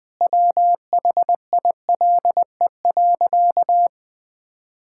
Morse code, heard as a keyed tone, about 20 wpm, 700 Hz. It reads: WHILE.
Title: Sentences with 4-Word Phrases - 20wpm